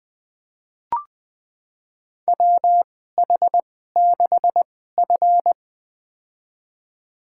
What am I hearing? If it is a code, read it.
Morse code, heard as a keyed tone, about 20 wpm, spaced as for 15 wpm, 700 Hz. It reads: WH6F
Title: USA Call Signs - 4 Characters - 2x Character Spacing - 20wpm